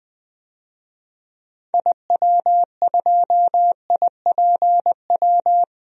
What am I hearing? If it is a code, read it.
Morse code, heard as a keyed tone, about 20 wpm, 700 Hz. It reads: IW2IPW
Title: DXCC-100 Call Signs - 6 Characters - 20wpm